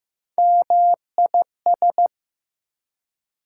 Morse code, heard as a keyed tone, about 15 wpm, 700 Hz. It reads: MIS
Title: Prefixes and Suffixes 15wpm